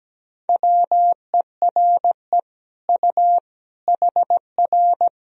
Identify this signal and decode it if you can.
Morse code, heard as a keyed tone, about 17 wpm, 700 Hz. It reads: WERE U HR
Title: Sentences from Top 200 Words and Abbreviations 17wpm